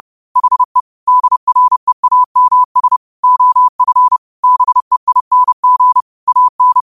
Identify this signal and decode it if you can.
Morse code, heard as a keyed tone, about 30 wpm, 1000 Hz. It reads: HE DREAMS OF BEING AN
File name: qotdfiles / 30wpm